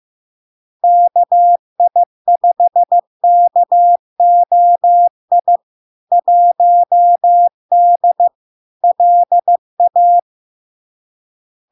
Morse code, heard as a keyed tone, about 15 wpm, 700 Hz. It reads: KI5KOI 1D LA
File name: LARK-FD-2022-Log-in-CW-15wpm-700-Hz